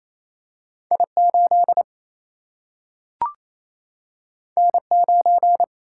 Morse code, heard as a keyed tone, about 28 wpm, 700 Hz. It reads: I8 N9
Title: Single-Letter Single-Number 28wpm